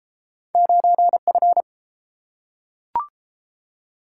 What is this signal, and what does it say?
Morse code, heard as a keyed tone, about 31 wpm, 700 Hz. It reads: 9F